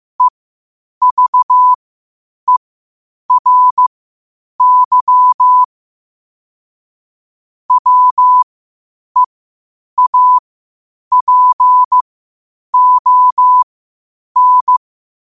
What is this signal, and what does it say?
Morse code, heard as a keyed tone, about 15 wpm, spaced as for 8 wpm, 1000 Hz. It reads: EVERY WEAPON
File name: qotdfiles / 8wpm